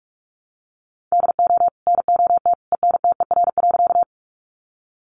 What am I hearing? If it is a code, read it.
Morse code, heard as a keyed tone, about 44 wpm, 700 Hz. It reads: DO NOT ENTER.
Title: Sentences from Top 1300 Words 45wpm